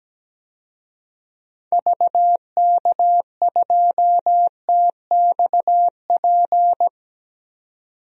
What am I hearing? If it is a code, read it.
Morse code, heard as a keyed tone, about 17 wpm, 700 Hz. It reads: VK2TXP
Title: DXCC-100 Call Signs - 6 Characters - 17wpm